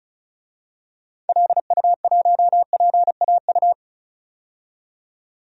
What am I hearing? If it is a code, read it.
Morse code, heard as a keyed tone, about 34 wpm, 700 Hz. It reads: LU1PAU
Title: DXCC-100 Call Signs - 6 Characters - 35wpm